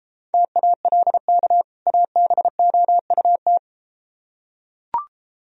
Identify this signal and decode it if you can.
Morse code, heard as a keyed tone, about 33 wpm, 700 Hz. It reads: TALK ABOUT